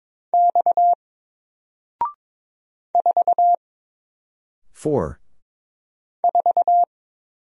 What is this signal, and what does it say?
Morse code, heard as a keyed tone, about 22 wpm, 700 Hz. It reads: X 4 4
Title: Single Letter-Number 22wpm